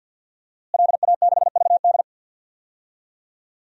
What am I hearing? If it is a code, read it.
Morse code, heard as a keyed tone, about 51 wpm, 700 Hz. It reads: LA6VD
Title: DXCC-100 Call Signs - 5 Characters - Repeated 3x - 50wpm